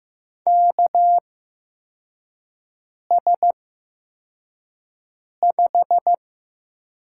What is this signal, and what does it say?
Morse code, heard as a keyed tone, about 15 wpm, spaced as for 4 wpm, 700 Hz. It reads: KS5
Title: USA Call Signs - 4 Characters - 8x Character Spacing - 15wpm